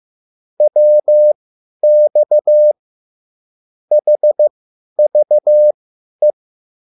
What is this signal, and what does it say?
Morse code, heard as a keyed tone, about 15 wpm, spaced as for 10 wpm, 600 Hz. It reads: WX HVE